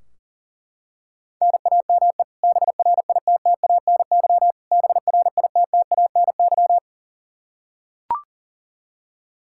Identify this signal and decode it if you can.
Morse code, heard as a keyed tone, about 40 wpm, 700 Hz. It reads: NAME BRITTANY BRITTANY
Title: Practice QSO - Exchange 1 - Beginning - 40wpm